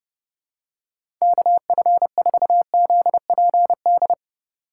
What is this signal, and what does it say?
Morse code, heard as a keyed tone, about 32 wpm, 700 Hz. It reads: KF4ZPD
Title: USA Call Signs - 6 Characters - 3x Character Spacing - 30wpm